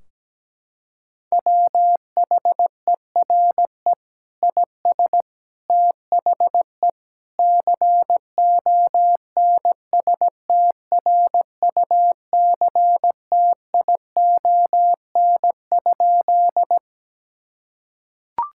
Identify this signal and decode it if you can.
Morse code, heard as a keyed tone, about 17 wpm, 700 Hz. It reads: WHERE IS THE CONSTRUCTION?
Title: Sentences from Top 1600 Words 17wpm